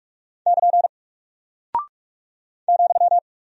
Morse code, heard as a keyed tone, about 44 wpm, 700 Hz. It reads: <KN> ,
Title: Single Character (Letter:Number:Punctuation:Prosign) 45wpm